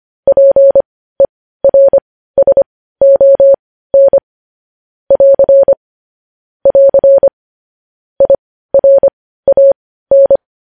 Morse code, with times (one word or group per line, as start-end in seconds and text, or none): 0.3-4.2 PERSON
5.1-5.7 <AR>
6.7-7.3 <AR>
8.2-10.4 IRAN